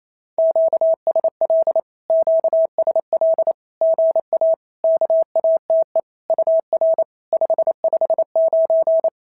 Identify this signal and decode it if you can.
Morse code, heard as a keyed tone, about 25 wpm, 650 Hz. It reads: QSL QSL GA KATE UR 559
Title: SOTA QSO Practice 28wpm